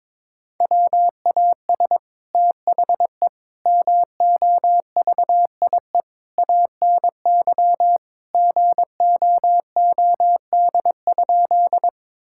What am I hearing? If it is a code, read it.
Morse code, heard as a keyed tone, about 22 wpm, 700 Hz. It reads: WAS THE MOVIE ANY GOOD?
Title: Sentences from Top 1000 Words - Review Part 5 - 22wpm